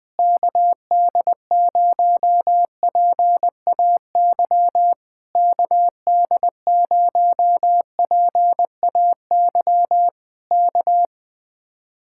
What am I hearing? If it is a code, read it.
Morse code, heard as a keyed tone, about 20 wpm, 700 Hz. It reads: KD0PAY KD0PAY K
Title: Speed-Racing Common QSO Phrases Encore - 20wpm